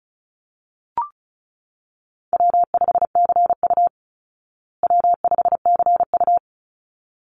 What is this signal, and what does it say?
Morse code, heard as a keyed tone, about 34 wpm, 700 Hz. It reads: W5CU W5CU
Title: Easy Call Signs Repeated 2x - 35wpm